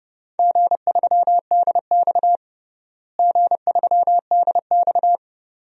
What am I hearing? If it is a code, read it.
Morse code, heard as a keyed tone, about 30 wpm, 700 Hz. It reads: G3DX G3DX
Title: DXCC-100 Call Signs - 4 Characters - Repeated 2x - 30wpm